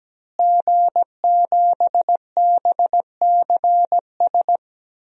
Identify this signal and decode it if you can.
Morse code, heard as a keyed tone, about 17 wpm, 700 Hz. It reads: G7BCS
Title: DXCC-100 Call Signs - 5 Characters - Repeated 3x - 17wpm